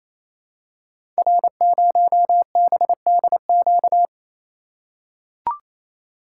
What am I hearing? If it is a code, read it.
Morse code, heard as a keyed tone, about 27 wpm, 700 Hz. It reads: R0BDQ
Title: Code Groups - 5 Letter:Numbers - 28wpm